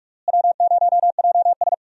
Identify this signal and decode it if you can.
Morse code, heard as a keyed tone, about 46 wpm, 700 Hz. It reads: W0JS
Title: USA Call Signs - 4 Characters - Repeated 3x - 45wpm